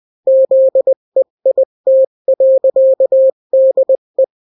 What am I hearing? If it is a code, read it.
Morse code, heard as a keyed tone, about 20 wpm, 530 Hz. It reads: ZEIT.DE
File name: daily / 18wpm